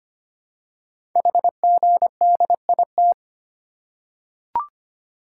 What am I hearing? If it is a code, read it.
Morse code, heard as a keyed tone, about 25 wpm, 700 Hz. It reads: HGDIT
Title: Code Groups - 5 Letters - 25wpm